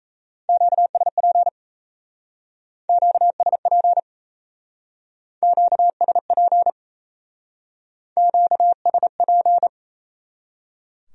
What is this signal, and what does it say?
Morse code, heard as a keyed tone, about 33 wpm, 700 Hz. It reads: QSP QSP QSP QSP
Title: Speed-Racing Q-Codes - 28wpm